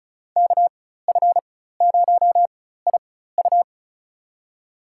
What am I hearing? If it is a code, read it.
Morse code, heard as a keyed tone, about 36 wpm, spaced as for 16 wpm, 700 Hz. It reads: KF0IU
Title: USA Call Signs - 4x Character Spacing - 35wpm